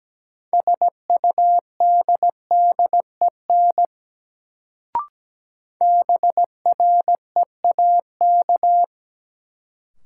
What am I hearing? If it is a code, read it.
Morse code, heard as a keyed tone, about 17 wpm, 700 Hz. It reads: SUDDEN BREAK
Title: Sets of 4 Words - New Word Drill - 17wpm